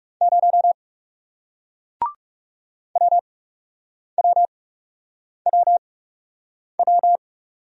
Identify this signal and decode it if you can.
Morse code, heard as a keyed tone, about 38 wpm, spaced as for 8 wpm, 700 Hz. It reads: 0 WWWW